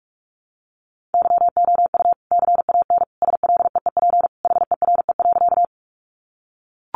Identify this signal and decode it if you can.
Morse code, heard as a keyed tone, about 44 wpm, 700 Hz. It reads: YOU CAN SLEEP HERE.
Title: Sentences from Top 500 Words 45wpm